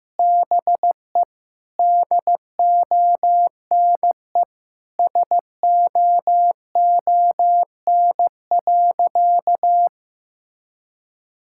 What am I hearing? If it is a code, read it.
Morse code, heard as a keyed tone, about 15 wpm, 700 Hz. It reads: BE DONE SOON.
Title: Sentences from Top 1600 Words 15wpm